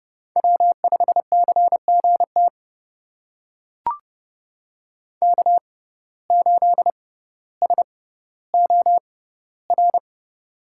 Morse code, heard as a keyed tone, about 30 wpm, spaced as for 15 wpm, 700 Hz. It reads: W5CGT K8SOR
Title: USA Call Signs - 5 Characters - 6x Character Spacing - 30wpm